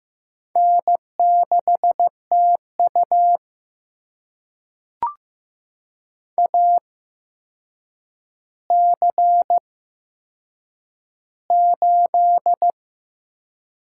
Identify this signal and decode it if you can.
Morse code, heard as a keyed tone, about 15 wpm, spaced as for 7 wpm, 700 Hz. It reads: N6TU AC8
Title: USA Call Signs - 4 Characters - 8x Character Spacing - 15wpm